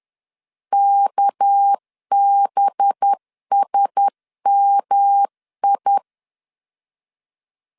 Morse code, heard as a keyed tone, about 10 wpm, 800 Hz. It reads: KBSMI
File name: CW / DET / 1 6 1500C S C 10wpm 4252